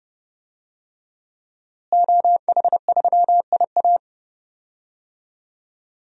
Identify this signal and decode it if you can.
Morse code, heard as a keyed tone, about 30 wpm, 700 Hz. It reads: OH3IA
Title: DXCC-100 Call Signs - 5 Characters - Repeated 3x - 30wpm